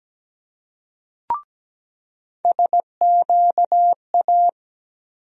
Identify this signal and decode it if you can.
Morse code, heard as a keyed tone, about 17 wpm, 700 Hz. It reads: SQA
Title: USA Call Sign Postfixes - 3 Letters 17wpm